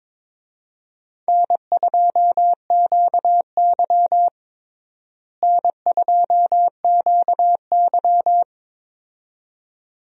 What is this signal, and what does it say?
Morse code, heard as a keyed tone, about 22 wpm, 700 Hz. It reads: N2QY N2QY
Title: USA Call Signs - 4 Characters - Repeated 2x - 22wpm